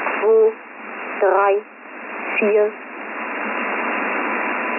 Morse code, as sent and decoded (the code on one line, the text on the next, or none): none